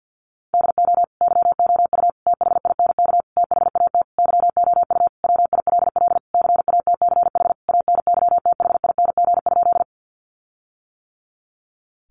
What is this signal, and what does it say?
Morse code, heard as a keyed tone, about 50 wpm, 700 Hz. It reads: DO YOU THINK THAT YOU WILL CATCH ANYTHING?
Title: Sentences from Top 1500 Words 50wpm